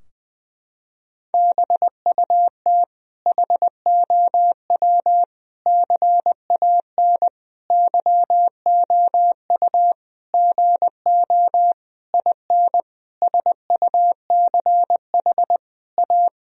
Morse code, heard as a keyed tone, about 21 wpm, 700 Hz. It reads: BUT HOW CAN YOU GO IN SUCH A
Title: Sentences from Top 1000 Words - Review Part 2 - 20wpm